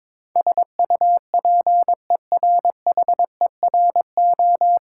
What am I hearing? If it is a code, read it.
Morse code, heard as a keyed tone, about 22 wpm, 690 Hz. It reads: SUPERHERO
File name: Difficult Words with Hints - 22wpm